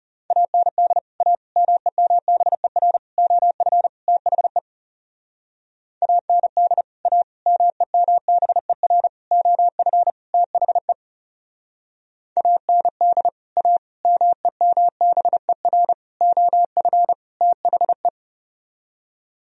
Morse code, heard as a keyed tone, about 37 wpm, 700 Hz. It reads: AND A MEMBER OF THE AND A MEMBER OF THE AND A MEMBER OF THE
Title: Speed-Racing - Sets of 5 Words 30wpm